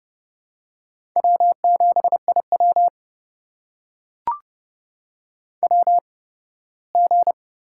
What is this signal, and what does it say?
Morse code, heard as a keyed tone, about 30 wpm, spaced as for 18 wpm, 700 Hz. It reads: W7IW WG
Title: USA Call Signs - 4 Characters - 8x Character Spacing - 30wpm